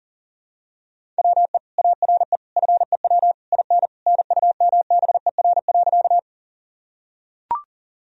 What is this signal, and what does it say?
Morse code, heard as a keyed tone, about 37 wpm, 700 Hz. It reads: WE ARE FEW IN NUMBER.
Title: Sentences from Top 1000 Words - Review Part 1 - 40wpm